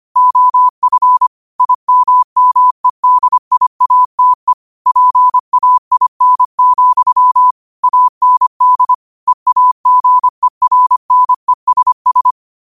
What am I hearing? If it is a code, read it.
Morse code, heard as a keyed tone, about 25 wpm, 1000 Hz. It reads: OF IMMEDIATE PAIN, AND EAGERNESS